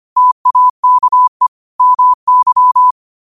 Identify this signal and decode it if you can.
Morse code, heard as a keyed tone, about 25 wpm, 1000 Hz. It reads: TAKE MY
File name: qotdfiles / 25wpm